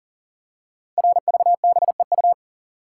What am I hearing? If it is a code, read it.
Morse code, heard as a keyed tone, about 38 wpm, 690 Hz. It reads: RVBEU
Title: Code Groups - 5 Letters - 40wpm